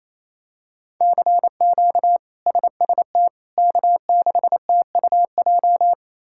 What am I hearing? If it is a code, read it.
Morse code, heard as a keyed tone, about 28 wpm, 700 Hz. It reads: CQ SST K6TUJ
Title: K1USN SST QSO Practice - 28wpm